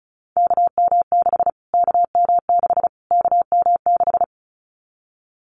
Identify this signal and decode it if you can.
Morse code, heard as a keyed tone, about 34 wpm, 700 Hz. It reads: KM6 KM6 KM6
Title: Callsign Prefixes for Top 25 DXCC - Repeated 3x - 35wpm